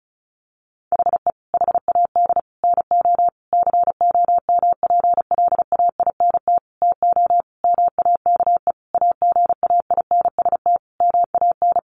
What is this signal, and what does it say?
Morse code, heard as a keyed tone, about 34 wpm, 700 Hz. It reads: HE HAD NO COMPLAINT TO MAKE AGAINST MAN
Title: Sentences from Top 1400 Words 35wpm